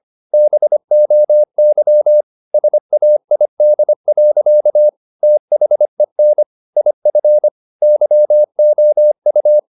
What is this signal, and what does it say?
Morse code, heard as a keyed tone, about 25 wpm, 600 Hz. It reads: BOY SAID. THEN IF YOU
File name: the old man and the sea part 1 25wpm